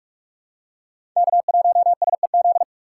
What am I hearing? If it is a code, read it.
Morse code, heard as a keyed tone, about 46 wpm, 700 Hz. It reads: K1SEZ
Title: USA Call Signs - 2x Character Spacing - 45wpm